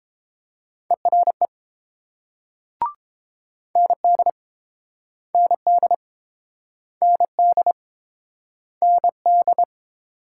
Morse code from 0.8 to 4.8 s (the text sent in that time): ERE ND